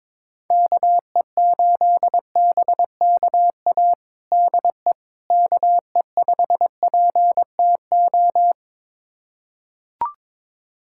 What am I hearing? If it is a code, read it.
Morse code, heard as a keyed tone, about 22 wpm, 700 Hz. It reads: KE8BKA DE KE5PTO
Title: USA Callsign DE Callsign - 6 Characters - 22wpm